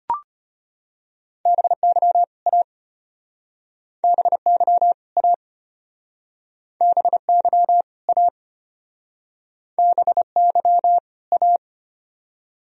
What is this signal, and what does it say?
Morse code, heard as a keyed tone, about 36 wpm, 700 Hz. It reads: BY A BY A BY A BY A